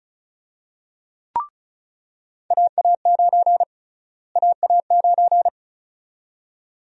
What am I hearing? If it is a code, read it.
Morse code, heard as a keyed tone, about 36 wpm, 700 Hz. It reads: AA9 AA9
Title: USA Call Sign Prefixes - Repeated 2x - 35wpm